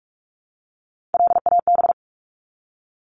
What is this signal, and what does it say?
Morse code, heard as a keyed tone, about 44 wpm, 700 Hz. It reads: LAB
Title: Top 1751-2000 Words 45wpm